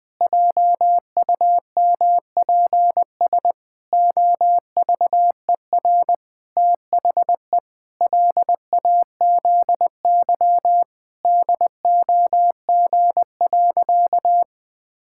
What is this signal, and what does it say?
Morse code, heard as a keyed tone, about 20 wpm, 700 Hz. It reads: JUMPS OVER THE LAZY DOG.
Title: Sentences with 8-Word Phrases - Spoken Beginning - 20wpm